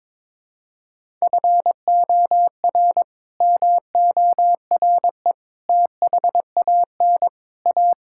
Morse code, heard as a keyed tone, about 22 wpm, 700 Hz. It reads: FOR MORE THAN A
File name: practice / 22wpm / Build A Phrase - 5-Word Phrases - 22wpm